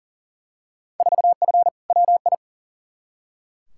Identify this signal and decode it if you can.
Morse code, heard as a keyed tone, about 37 wpm, 700 Hz. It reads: 4F WI
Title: Field Day Exchanges Encore - 40wpm